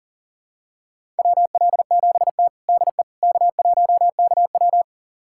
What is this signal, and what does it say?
Morse code, heard as a keyed tone, about 40 wpm, 700 Hz. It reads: WL7T DE K1KW
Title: USA Callsign DE Callsign - 4 Characters - 40wpm